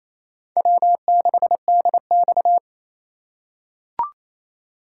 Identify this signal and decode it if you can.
Morse code, heard as a keyed tone, about 27 wpm, 700 Hz. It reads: W6DX
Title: USA Call Signs - 4x Character Spacing - 28wpm